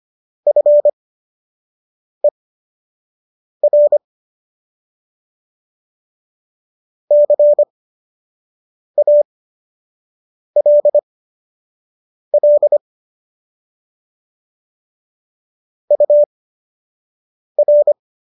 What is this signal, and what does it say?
Morse code, heard as a keyed tone, about 25 wpm, spaced as for 6 wpm, 590 Hz. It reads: FER CALL UR